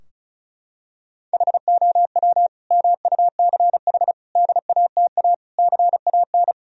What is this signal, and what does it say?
Morse code, heard as a keyed tone, about 34 wpm, 700 Hz. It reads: HOW MUCH DATA CAN
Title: Sentences from Top 1000 Words - Review Part 4 - 35wpm